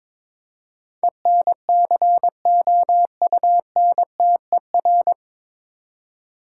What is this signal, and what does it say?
Morse code, heard as a keyed tone, about 22 wpm, 700 Hz. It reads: ENCOUNTER
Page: Morse Code Ninja practice 22wpm